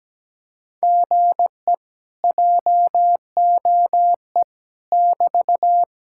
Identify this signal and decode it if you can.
Morse code, heard as a keyed tone, about 17 wpm, 700 Hz. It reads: GE JOE <BT>
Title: Speed-Racing Common QSO Phrases - 17wpm